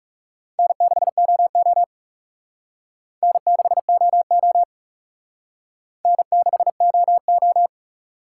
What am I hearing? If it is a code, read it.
Morse code, heard as a keyed tone, about 40 wpm, 700 Hz. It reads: N6OO N6OO N6OO